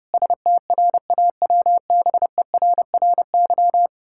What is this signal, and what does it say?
Morse code, heard as a keyed tone, about 30 wpm, 700 Hz. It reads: STRAWBERRY
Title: Pangram Challenge v2 - 30wpm